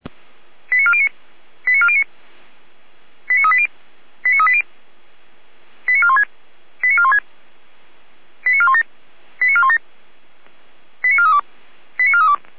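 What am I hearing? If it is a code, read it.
Five-tone selective call (ZVEI), heard as quick runs of five tones, tones of about 70 ms. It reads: 89409 89409 89380 89380 89416 89416 89417 89417 89432 89432
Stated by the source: ZVEI five-tone call 89350 received on an SDR.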